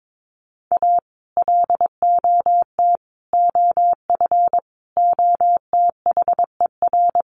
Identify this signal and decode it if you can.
Morse code, heard as a keyed tone, about 22 wpm, 700 Hz. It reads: A LOT OF OTHER
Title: Sets of 5 Words Encore2 22wpm